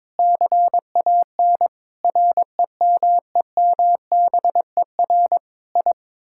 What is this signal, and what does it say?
Morse code, heard as a keyed tone, about 22 wpm, 700 Hz. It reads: CAN REMEMBER I
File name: practice / 22wpm / Sets of 8 Words - 22wpm